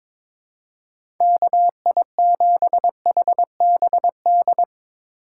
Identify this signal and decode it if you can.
Morse code, heard as a keyed tone, about 22 wpm, 700 Hz. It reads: KI7HBD
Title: USA Call Signs - 6 Characters - 4x Character Spacing - 22wpm